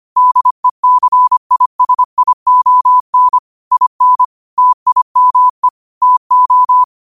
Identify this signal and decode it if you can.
Morse code, heard as a keyed tone, about 25 wpm, 1000 Hz. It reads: DECISION IN TIME TO
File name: qotdfiles / 25wpm